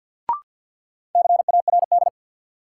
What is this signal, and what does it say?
Morse code, heard as a keyed tone, about 46 wpm, 700 Hz. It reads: CARD
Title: Sentences from Top 600 Words - New Word Drill - 50wpm